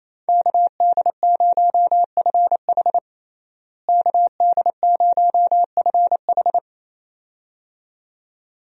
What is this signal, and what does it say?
Morse code, heard as a keyed tone, about 28 wpm, 700 Hz. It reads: KD0FH KD0FH
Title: USA Call Signs - 5 Characters - Repeated 2x - 28wpm